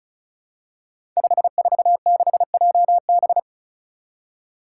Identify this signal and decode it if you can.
Morse code, heard as a keyed tone, about 35 wpm, 700 Hz. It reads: 546JB
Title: Five Challenging Letter-Number Combinations 35wpm